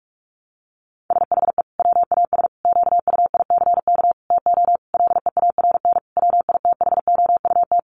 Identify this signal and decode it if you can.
Morse code, heard as a keyed tone, about 44 wpm, 700 Hz. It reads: SHE WAS QUICK TO LEARN WITHOUT
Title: Sentences from Top 1200 Words 45wpm